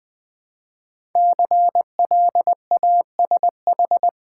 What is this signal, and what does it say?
Morse code, heard as a keyed tone, about 20 wpm, 700 Hz. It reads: CLASH
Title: 5 Letter Words - First 3 Letters Given - 20wpm